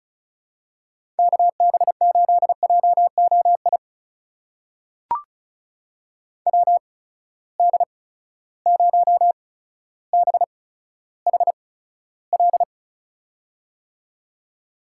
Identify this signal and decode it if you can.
Morse code, heard as a keyed tone, about 36 wpm, spaced as for 15 wpm, 700 Hz. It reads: KB8JOI WD0BHL
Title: USA Call Signs - 6 Characters - 8x Character Spacing - 35wpm